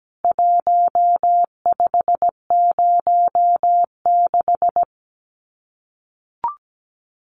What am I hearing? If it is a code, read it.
Morse code, heard as a keyed tone, about 17 wpm, 700 Hz. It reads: 1506
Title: CWT Exchange 17wpm